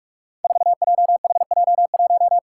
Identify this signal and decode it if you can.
Morse code, heard as a keyed tone, about 46 wpm, 700 Hz. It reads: 4JHJ1
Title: Five Challenging Letter-Number Combinations 45wpm